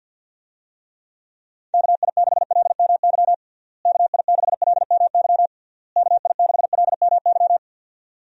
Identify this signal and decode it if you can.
Morse code, heard as a keyed tone, about 52 wpm, 700 Hz. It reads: KI6LMY KI6LMY KI6LMY
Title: USA Call Signs - 6 Characters - Repeated 3x - 50wpm